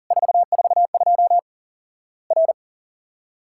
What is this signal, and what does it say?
Morse code, heard as a keyed tone, about 40 wpm, 700 Hz. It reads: 442 R
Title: Worked All Europe DX Contest - QTC Exchange - 40wpm